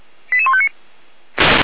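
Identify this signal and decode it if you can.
Five-tone selective call (ZVEI), heard as quick runs of five tones, tones of about 70 ms. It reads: 88159
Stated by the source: ZVEI five-tone selective call received via SDR, archived by the Feuerwehr Orlamünde fire brigade.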